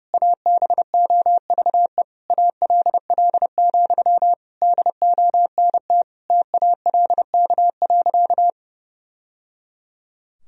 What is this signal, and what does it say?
Morse code, heard as a keyed tone, about 30 wpm, 700 Hz. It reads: ABOVE ALL, DONT TALK.
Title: Sentences from Top 1000 Words - Review Part 2 - 30wpm